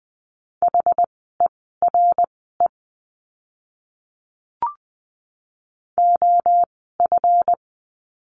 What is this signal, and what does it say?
Morse code, heard as a keyed tone, about 20 wpm, spaced as for 15 wpm, 700 Hz. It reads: HERE OF
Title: 192 Lesson 31f - Letter v - Words 20wpm